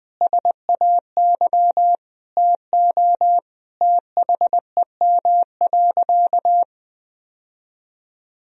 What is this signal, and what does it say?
Morse code, heard as a keyed tone, about 20 wpm, 700 Hz. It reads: SAY TO THEM.
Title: Sentences from Top 1000 Words - Review Part 1 - 20wpm